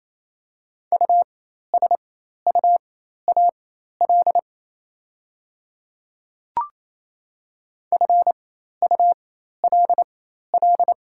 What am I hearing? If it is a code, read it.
Morse code, heard as a keyed tone, about 28 wpm, spaced as for 13 wpm, 690 Hz. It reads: USUAL FULL
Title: 4x Character Spacing - Top 500 Words - 28wpm